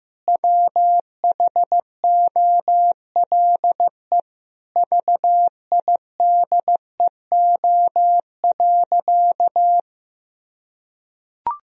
Morse code, heard as a keyed tone, about 15 wpm, 700 Hz. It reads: WHOLE VIDEO.
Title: Sentences from Top 1000 Words - Review Part 4 - 15wpm